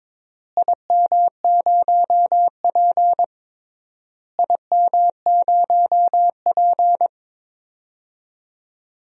Morse code, heard as a keyed tone, about 22 wpm, 700 Hz. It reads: IM0P IM0P
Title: DXCC-100 Call Signs - 4 Characters - Repeated 3x - 22wpm